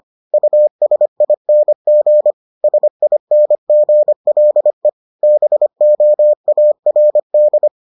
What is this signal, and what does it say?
Morse code, heard as a keyed tone, about 25 wpm, 600 Hz. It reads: USING SINGLE BOARD